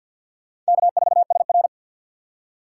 Morse code, heard as a keyed tone, about 61 wpm, 700 Hz. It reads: K4SR
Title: Speed Racing - USA 1x2 Call Signs 50wpm